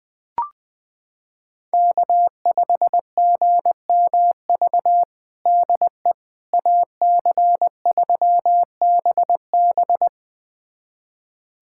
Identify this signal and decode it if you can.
Morse code, heard as a keyed tone, about 20 wpm, 700 Hz. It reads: K5GMV DE AC3BB
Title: USA Callsign DE Callsign - 5 Characters - 20wpm